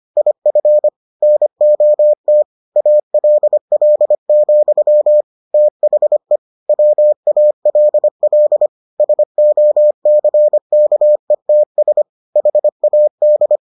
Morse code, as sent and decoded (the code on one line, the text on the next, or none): .. ..-. / -. --- - / .- .-.. .-.. --..-- / - .... / .-- .- .-.. .-.. / ... --- -.-. -.- . - ... / .... .- -..
IF NOT ALL, THE WALL SOCKETS HAD